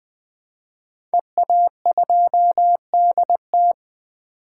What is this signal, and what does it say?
Morse code, heard as a keyed tone, about 20 wpm, 700 Hz. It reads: EA2DT
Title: 2023 SOTA Chaser Call Signs - 20wpm